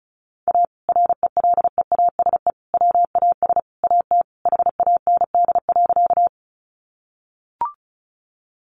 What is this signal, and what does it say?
Morse code, heard as a keyed tone, about 34 wpm, 700 Hz. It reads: A RELEASE WAS AT HAND.
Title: Sentences from Top 1100 Words 35wpm